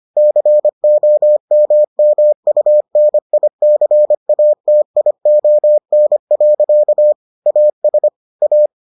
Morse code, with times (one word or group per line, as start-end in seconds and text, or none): 0.2-7.1 COMMUNICATION.
7.5-8.1 AS
8.4-8.7 A